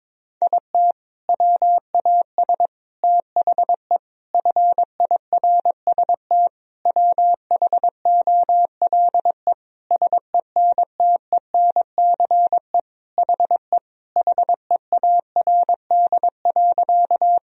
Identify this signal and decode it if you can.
Morse code, heard as a keyed tone, about 22 wpm, 700 Hz. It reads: IT WAS THE FIRST WHOLE SENTENCE HE HEARD.
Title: Sentences from Top 1000 Words - Review Part 1 - 22wpm